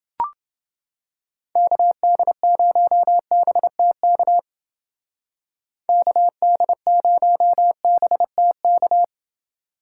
Morse code, heard as a keyed tone, about 28 wpm, 700 Hz. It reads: KD0BTK KD0BTK